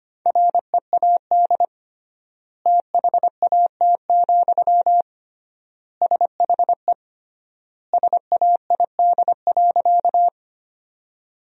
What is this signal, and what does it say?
Morse code, heard as a keyed tone, about 25 wpm, 700 Hz. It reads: READ THAT, SHE SAID.
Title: 3x Word-Spacing - Sentences from Top 200 Words - 25wpm